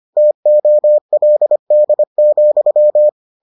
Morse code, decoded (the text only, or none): TOLD,